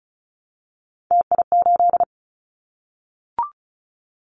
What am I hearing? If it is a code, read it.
Morse code, heard as a keyed tone, about 36 wpm, 700 Hz. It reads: TI8
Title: Callsign Prefixes for Top 100 DXCC - 35wpm